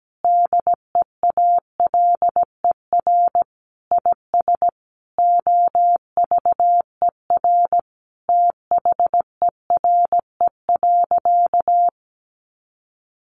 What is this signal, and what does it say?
Morse code, heard as a keyed tone, about 17 wpm, 700 Hz. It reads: DEALER IS OVER THERE.
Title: Sentences from Top 1100 Words 17wpm